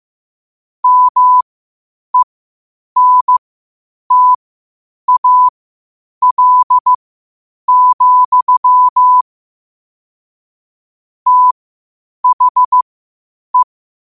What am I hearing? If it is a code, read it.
Morse code, heard as a keyed tone, about 15 wpm, spaced as for 8 wpm, 1000 Hz. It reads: MENTAL, THE
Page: Quotes for Thu, 14 Aug 2025 in Morse Code at 8 words per minute.